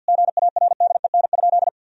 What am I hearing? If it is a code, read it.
Morse code, heard as a keyed tone, about 50 wpm, 700 Hz. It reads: GARDEN?
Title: Sentences from Top 1000 Words - Review Part 4 - 50wpm